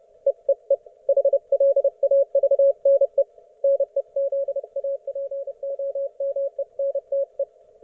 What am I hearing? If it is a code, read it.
Morse code, heard as a keyed tone, about 30 wpm, 550 Hz. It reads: EEE HLAVNE NEZAPOMENTE
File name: Archiv / CUC181 / ok1fhi cul181